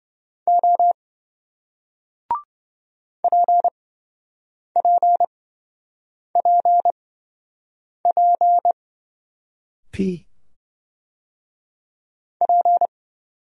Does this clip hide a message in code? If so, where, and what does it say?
Yes: Morse code from 0.5 to 12.9 s, heard as a keyed tone, about 28 wpm, spaced as for 7 wpm, 700 Hz. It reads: O PPPP P